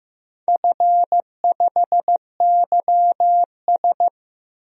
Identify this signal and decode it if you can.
Morse code, heard as a keyed tone, about 15 wpm, 700 Hz. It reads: F5YS
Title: DXCC-100 Call Signs - 4 Characters - Repeated 2x - 15wpm